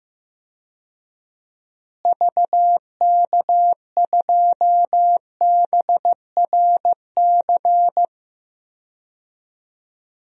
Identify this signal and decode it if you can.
Morse code, heard as a keyed tone, about 15 wpm, 700 Hz. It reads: VK2BRC
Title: DXCC-100 Call Signs - 6 Characters - Repeated 3x - 15wpm